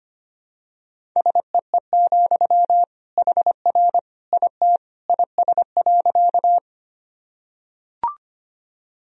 Morse code, heard as a keyed tone, about 25 wpm, 700 Hz. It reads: SEE, HR IT IS.
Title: Sentences from Top 200 Words and Abbreviations 25wpm